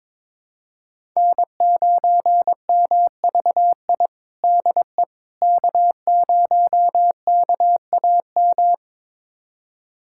Morse code, heard as a keyed tone, about 22 wpm, 700 Hz. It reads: N9MVI DE K0KAM
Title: USA Callsign DE Callsign - 5 Characters - 22wpm